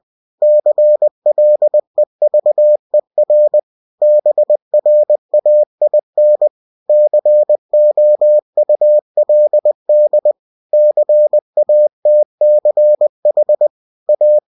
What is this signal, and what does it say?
Morse code, heard as a keyed tone, about 20 wpm, 600 Hz. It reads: CLEVER BRAIN COULD CATCH A
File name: cw-books / winnie-the-pooh / winnie-20wpm-page-0036